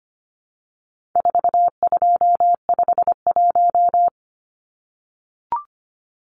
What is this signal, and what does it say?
Morse code, heard as a keyed tone, about 25 wpm, 700 Hz. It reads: 4251
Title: Numbers - 4-Digit 25wpm